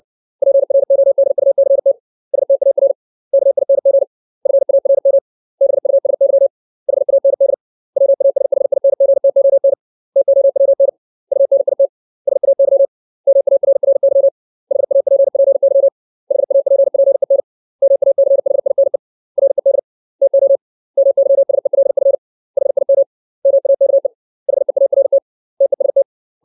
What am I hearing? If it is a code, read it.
Morse code, heard as a keyed tone, about 60 wpm, 550 Hz. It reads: PRODUCT STAR KING FARM BUSY HAND WASHINGTON TOWN UNIT SAY MARRY HAPPY HAPPEN MACHINE RED TO MYSELF HIM MAKE HEART TEST
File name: 632 words / 632 Words 60WPM